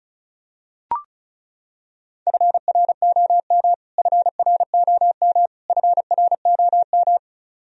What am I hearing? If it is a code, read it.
Morse code, heard as a keyed tone, about 36 wpm, 700 Hz. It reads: FROM FROM FROM
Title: Top 100 Words Repeated 35wpm